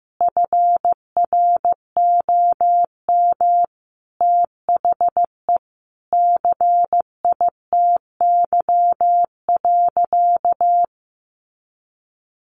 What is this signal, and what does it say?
Morse code, heard as a keyed tone, about 15 wpm, 700 Hz. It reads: FROM THE CITY.
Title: Sentences from Top 1200 Words 15wpm